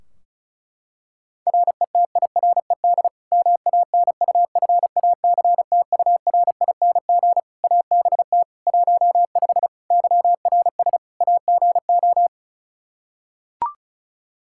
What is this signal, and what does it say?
Morse code, heard as a keyed tone, about 36 wpm, 700 Hz. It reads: RETIRED MANUFACTURING ABT 15 YRS AGO
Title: Practice QSO - Exchange 3 - More Info - 35wpm